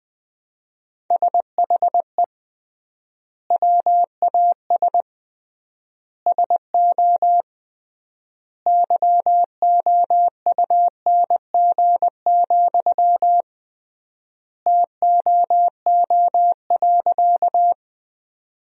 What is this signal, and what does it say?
Morse code, heard as a keyed tone, about 20 wpm, 700 Hz. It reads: SHE WAS SO YOUNG, TOO.
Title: 3x Word-Spacing - Sentences from Top 400 Words - 20wpm